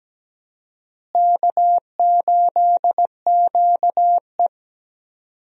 Morse code, heard as a keyed tone, about 17 wpm, 700 Hz. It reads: K8QE
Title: USA Call Signs - 4 Characters - Repeated 2x - 17wpm